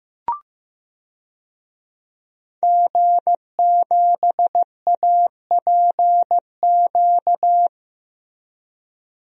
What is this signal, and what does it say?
Morse code, heard as a keyed tone, about 15 wpm, 700 Hz. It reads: G7APQ
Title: DXCC-100 Call Signs - 5 Characters - Repeated 3x - 15wpm